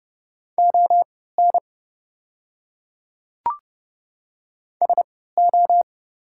Morse code, heard as a keyed tone, about 29 wpm, spaced as for 17 wpm, 690 Hz. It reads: ON SO
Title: 3x Character Spacing - Top 300 Words - 30wpm